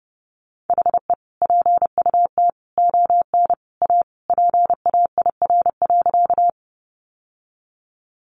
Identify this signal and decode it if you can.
Morse code, heard as a keyed tone, about 28 wpm, 700 Hz. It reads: HE PUT ON A PAIR.
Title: Sentences from Top 1200 Words 30wpm